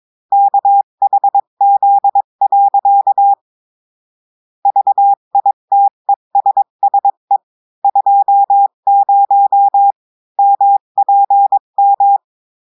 Morse code, heard as a keyed tone, about 22 wpm, 800 Hz. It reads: KHZ. VITESSE 20 MPM
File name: cours-du-mercredi / audios / cours-853-du-2-mai-2024-etxte chiffre et texte clair-a-20-wpm